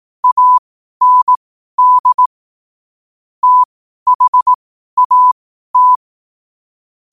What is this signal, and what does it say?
Morse code, heard as a keyed tone, about 18 wpm, spaced as for 12 wpm, 1000 Hz. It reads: AND THAT
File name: qotdfiles / 12wpm